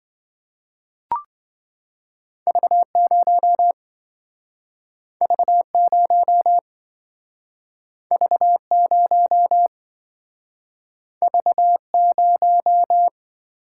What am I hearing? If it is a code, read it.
Morse code, heard as a keyed tone, about 26 wpm, 700 Hz. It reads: V0 V0 V0 V0